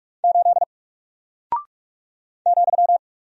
Morse code, heard as a keyed tone, about 44 wpm, 700 Hz. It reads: 8 ,
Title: Single Character (Letter:Number:Punctuation:Prosign) Alt 45wpm